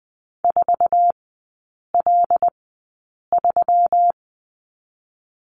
Morse code, heard as a keyed tone, about 19 wpm, 700 Hz. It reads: 4 L 3
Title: 121 Lesson 20e - Number 3 - Three Characters 20wpm